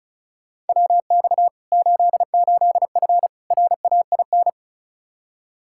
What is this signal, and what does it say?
Morse code, heard as a keyed tone, about 36 wpm, 700 Hz. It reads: WX 88F RAIN
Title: Practice QSO - Exchange 2 - Info - 35wpm